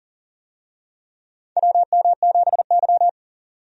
Morse code, heard as a keyed tone, about 40 wpm, 700 Hz. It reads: WM7Y
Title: USA Call Signs - 4 Characters - 40wpm